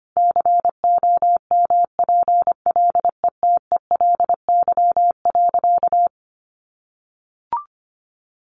Morse code, heard as a keyed tone, about 25 wpm, 700 Hz. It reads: COMPLETELY.
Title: Sentences from Top 1400 Words 25wpm